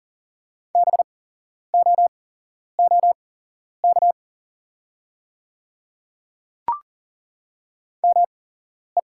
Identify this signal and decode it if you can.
Morse code, heard as a keyed tone, about 40 wpm, spaced as for 11 wpm, 700 Hz. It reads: BOOK ME
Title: 8x Character Spacing - Top 500 Words - 40wpm